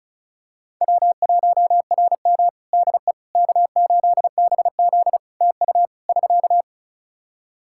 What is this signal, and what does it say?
Morse code, heard as a keyed tone, about 34 wpm, 700 Hz. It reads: W1RM DE K8BZ TU <SK>
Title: Practice QSO - Exchange 4 - Ending - 35wpm